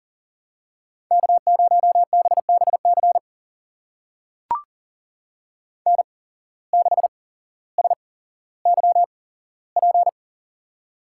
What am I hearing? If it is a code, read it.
Morse code, heard as a keyed tone, about 38 wpm, spaced as for 17 wpm, 700 Hz. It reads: K0BBC N6SYP
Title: USA Call Signs - 5 Characters - 8x Character Spacing - 40wpm